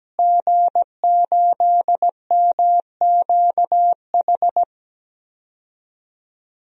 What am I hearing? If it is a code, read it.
Morse code, heard as a keyed tone, about 17 wpm, 700 Hz. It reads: G8MQH
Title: DXCC-100 Call Signs - 5 Characters - Repeated 3x - 17wpm